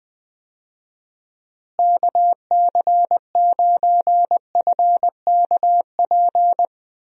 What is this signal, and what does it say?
Morse code, heard as a keyed tone, about 21 wpm, 700 Hz. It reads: KC9FKP
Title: USA Call Signs - 6 Characters - Repeated 2x - 20wpm